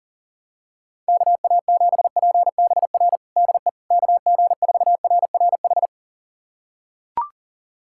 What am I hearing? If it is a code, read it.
Morse code, heard as a keyed tone, about 40 wpm, 700 Hz. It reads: KA7PBR DE KG4RRH
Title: USA Callsign DE Callsign - 6 Characters - 40wpm